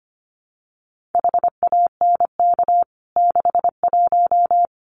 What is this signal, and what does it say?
Morse code, heard as a keyed tone, about 25 wpm, 700 Hz. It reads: HANK 61
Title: CWT Exchange 25wpm